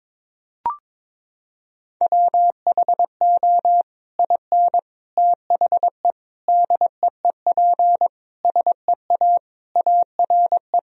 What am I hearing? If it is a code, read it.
Morse code, heard as a keyed tone, about 22 wpm, 700 Hz. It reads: WHO IN THE DEEP SEA ARE
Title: Sentences from Top 1000 Words - Review Part 3 - 22wpm